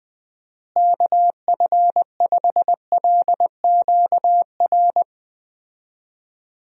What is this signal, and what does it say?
Morse code, heard as a keyed tone, about 20 wpm, 700 Hz. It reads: KF5LQR
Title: USA Call Signs - 6 Characters - Repeated 3x - 20wpm